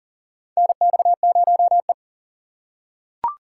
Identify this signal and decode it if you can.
Morse code, heard as a keyed tone, about 40 wpm, 700 Hz. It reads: NX0E
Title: USA Call Signs - 4 Characters - 3x Character Spacing - 40wpm